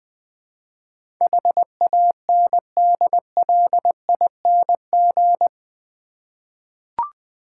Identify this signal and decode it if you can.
Morse code, heard as a keyed tone, about 20 wpm, 700 Hz. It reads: HANDLING
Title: Eight Letter Words - 20wpm